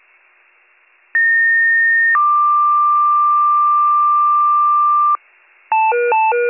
Two-tone page, heard as a one-second tone, then a three-second tone, about 1850 then 1180 Hz. It is heard from 1.1 to 5.2 s.